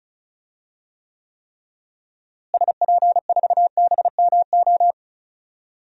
Morse code, heard as a keyed tone, about 34 wpm, 700 Hz. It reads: SP4BMO
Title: DXCC-100 Call Signs - 6 Characters - Repeated 3x - 35wpm